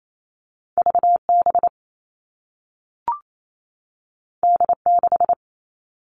Morse code, heard as a keyed tone, about 28 wpm, 700 Hz. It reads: VB D6
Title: All Cut Numbers - 2-Digit 28wpm